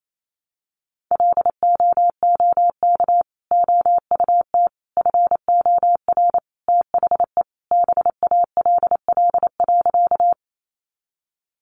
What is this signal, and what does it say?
Morse code, heard as a keyed tone, about 28 wpm, 700 Hz. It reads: LOOK OUT FOR THE BALL.
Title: Sentences from Top 900 Words 28wpm